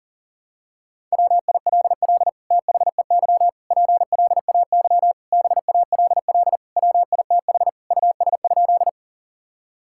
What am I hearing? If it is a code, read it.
Morse code, heard as a keyed tone, about 40 wpm, 700 Hz. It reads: WILL THEY PLAY BALL WITH US?
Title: Sentences from Top 1000 Words - Review Part 6 - 40wpm